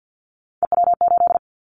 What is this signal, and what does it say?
Morse code, heard as a keyed tone, about 50 wpm, 700 Hz. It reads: EW8
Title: Callsign Prefixes for Top 100 DXCC - 50wpm